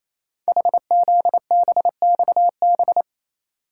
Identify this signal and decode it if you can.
Morse code, heard as a keyed tone, about 28 wpm, 700 Hz. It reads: HZBXB
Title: Code Groups - 5 Letters - 28wpm